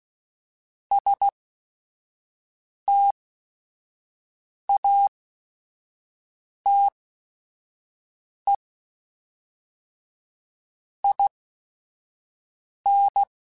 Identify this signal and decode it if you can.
Morse code, heard as a keyed tone, about 16 wpm, spaced as for 5 wpm, 800 Hz. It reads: STATE IN
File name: QSOs / QSOs-mp3s / WD0BGP 1@5